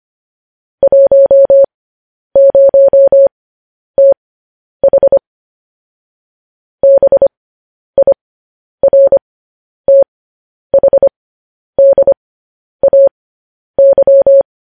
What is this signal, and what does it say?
Morse code, heard as a keyed tone, about 25 wpm, spaced as for 10 wpm, 550 Hz. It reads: 10TH BIRTHDAY
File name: podcast / News headlines-1x-25x10